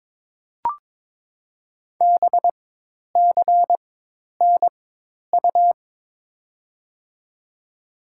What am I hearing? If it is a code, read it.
Morse code, heard as a keyed tone, about 22 wpm, 700 Hz. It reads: B C N U
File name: Top 90 QSO Elements - 4x Character Spacing - 22wpm